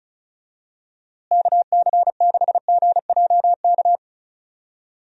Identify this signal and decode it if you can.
Morse code, heard as a keyed tone, about 36 wpm, 700 Hz. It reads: KC6GJK
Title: USA Call Signs - 6 Characters - Repeated 2x - 35wpm